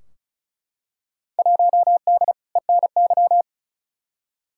Morse code, heard as a keyed tone, about 36 wpm, 700 Hz. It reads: 1D ENY
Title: Field Day Exchanges - 35wpm